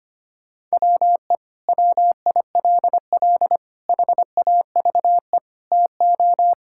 Morse code, heard as a keyed tone, about 25 wpm, 700 Hz. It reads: WE WILL HAVE TO
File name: practice / 25wpm / Sentences with 3-Word Phrases - 25wpm